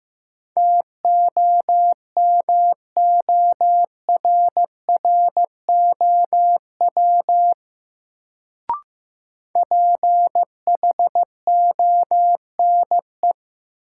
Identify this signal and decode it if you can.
Morse code, heard as a keyed tone, about 15 wpm, 700 Hz. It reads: TOMORROW PHONE
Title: Sentences from Top 1100 Words - New Word Drill - 15wpm